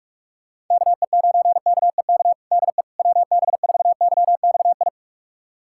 Morse code, heard as a keyed tone, about 44 wpm, 690 Hz. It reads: KE9KEK DE WB4YXI
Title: USA Callsign DE Callsign - 6 Characters - 45wpm